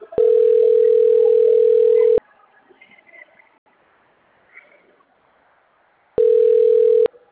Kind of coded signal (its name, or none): telephone call progress tone